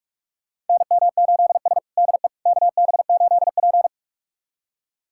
Morse code, heard as a keyed tone, about 46 wpm, 700 Hz. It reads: NM8S DE KB8P
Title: USA Callsign DE Callsign - 4 Characters - 45wpm